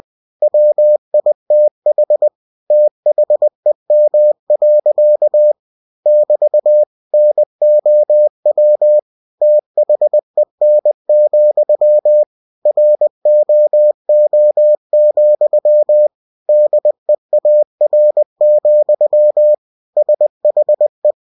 Morse code, heard as a keyed tone, about 20 wpm, 600 Hz. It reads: WITH THEM. <BT> NOW THEN, ROO, DEAR, SHE
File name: cw-books / winnie-the-pooh / winnie-20wpm-page-0071